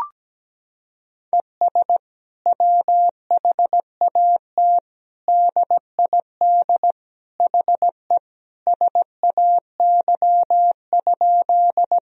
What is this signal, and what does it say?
Morse code, heard as a keyed tone, about 17 wpm, 700 Hz. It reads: ES WHAT DID HE SAY?
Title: Sentences from Top 200 Words and Abbreviations 17wpm